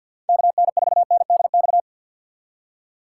Morse code, heard as a keyed tone, about 51 wpm, 700 Hz. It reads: KN4NDX
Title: USA Call Signs - 3x Character Spacing - 50wpm